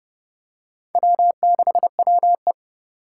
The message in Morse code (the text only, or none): W6WE